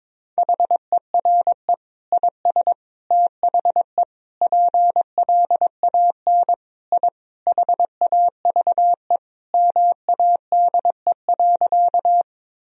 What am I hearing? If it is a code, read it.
Morse code, heard as a keyed tone, about 22 wpm, 700 Hz. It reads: HERE IS THE PLAN I HAVE MADE.
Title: Sentences from Top 1000 Words - Review Part 2 - 22wpm